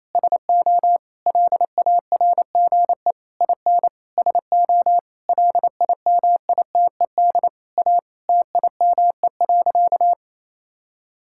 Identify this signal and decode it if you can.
Morse code, heard as a keyed tone, about 28 wpm, 700 Hz. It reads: SO LARGE IN SO LIMITED A TIME.
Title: Sentences from Top 1000 Words - Review Part 6 - 28wpm